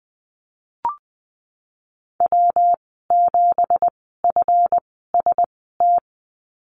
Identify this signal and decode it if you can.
Morse code, heard as a keyed tone, about 20 wpm, spaced as for 15 wpm, 700 Hz. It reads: W7FST
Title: 186 Lesson 30g - Letter b - Call Signs 20wpm